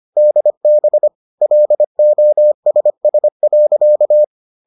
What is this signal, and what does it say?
Morse code, heard as a keyed tone, about 25 wpm, 600 Hz. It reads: DB LOSS.